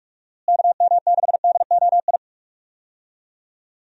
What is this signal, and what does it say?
Morse code, heard as a keyed tone, about 41 wpm, 700 Hz. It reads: KM6DOI
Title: USA Call Signs - 6 Characters - Repeated 3x - 45wpm